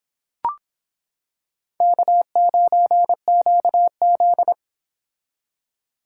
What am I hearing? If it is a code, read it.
Morse code, heard as a keyed tone, about 26 wpm, 700 Hz. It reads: K9QZ